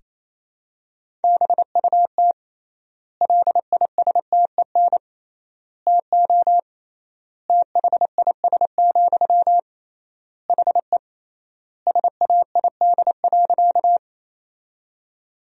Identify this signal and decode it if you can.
Morse code, heard as a keyed tone, about 28 wpm, 700 Hz. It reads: BUT LISTEN TO THIS, HE SAID.
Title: 3x Word-Spacing - Sentences from Top 400 Words - 28wpm